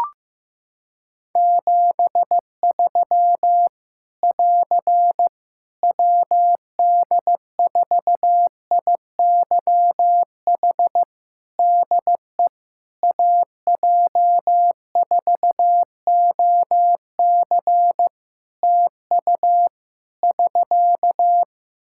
Morse code, heard as a keyed tone, about 15 wpm, 700 Hz. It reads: 73 <AR> WD4IYH DE AJ4OC TU <SK>
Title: Practice QSO - Exchange 4 - Ending - 15wpm